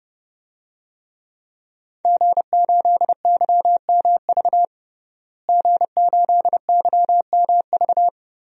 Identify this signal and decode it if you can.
Morse code, heard as a keyed tone, about 30 wpm, 700 Hz. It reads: G8YMV G8YMV
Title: DXCC-100 Call Signs - 5 Characters - Repeated 2x - 30wpm